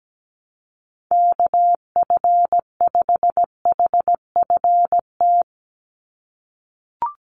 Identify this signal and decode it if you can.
Morse code, heard as a keyed tone, about 17 wpm, 700 Hz. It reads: KF5HFT
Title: USA Call Signs 17wpm